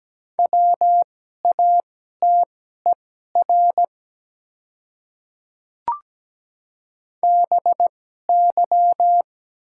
Morse code, heard as a keyed tone, about 17 wpm, spaced as for 12 wpm, 690 Hz. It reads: WATER BY
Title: 2x Character Spacing - Top 100 Words - 17wpm